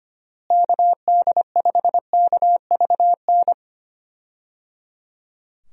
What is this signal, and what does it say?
Morse code, heard as a keyed tone, about 25 wpm, 700 Hz. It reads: KD5KVN
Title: USA Call Signs - 6 Characters - Repeated 3x - 25wpm